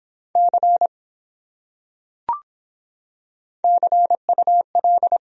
Morse code, heard as a keyed tone, about 25 wpm, 700 Hz. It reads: C CUL